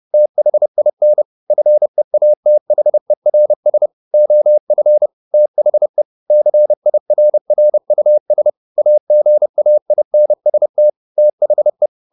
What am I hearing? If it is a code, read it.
Morse code, heard as a keyed tone, about 30 wpm, 600 Hz. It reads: THIN FEATHERS OF THE CIRRUS AGAINST THE
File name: the old man and the sea / the old man and the sea 1 30wpm